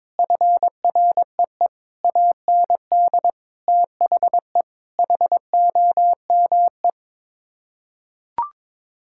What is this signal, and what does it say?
Morse code, heard as a keyed tone, about 22 wpm, 700 Hz. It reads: FREE AND THE HOME
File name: practice / 22wpm / Sets of 8 Words - 22wpm